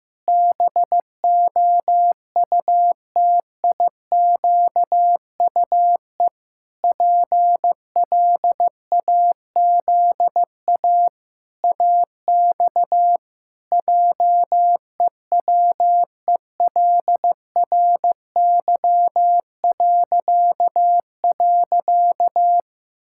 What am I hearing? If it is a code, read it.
Morse code, heard as a keyed tone, about 15 wpm, 700 Hz. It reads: BOUTIQUE PLAZA AX JEWELRY..
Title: Pangram Challenge v2 - 15wpm